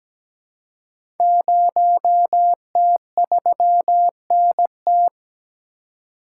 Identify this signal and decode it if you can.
Morse code, heard as a keyed tone, about 17 wpm, 700 Hz. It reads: 0T3NT
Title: Code Groups - 5 Letter:Numbers - 17wpm